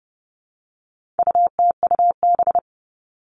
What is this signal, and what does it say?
Morse code, heard as a keyed tone, about 30 wpm, 700 Hz. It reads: UTUB
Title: All Cut Numbers - 4-Digit 30wpm